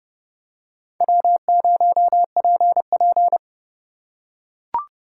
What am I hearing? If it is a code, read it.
Morse code, heard as a keyed tone, about 29 wpm, 700 Hz. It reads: W0PP